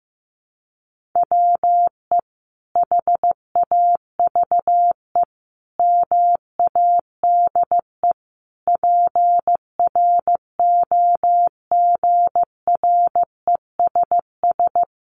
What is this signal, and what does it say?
Morse code, heard as a keyed tone, about 14 wpm, 700 Hz. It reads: WE HAVE MADE PROGRESS
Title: Sentences from Top 1400 Words 15wpm